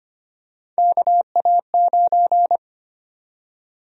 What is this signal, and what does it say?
Morse code, heard as a keyed tone, about 25 wpm, 700 Hz. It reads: KA9
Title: USA Call Sign Prefixes - 25wpm